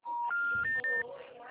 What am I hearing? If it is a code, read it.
Special information tone, with speech.